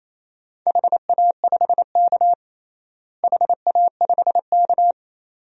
Morse code, heard as a keyed tone, about 28 wpm, 700 Hz. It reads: HA5K HA5K
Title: DXCC-100 Call Signs - 4 Characters - Repeated 2x - 28wpm